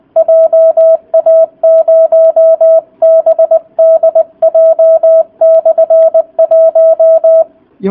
Morse code, heard as a keyed tone, about 20 wpm, 650 Hz. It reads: JA0BDJ/1